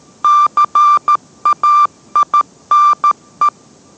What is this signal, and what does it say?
Morse code, heard as a keyed tone, about 13 wpm, 1180 Hz. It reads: CAINE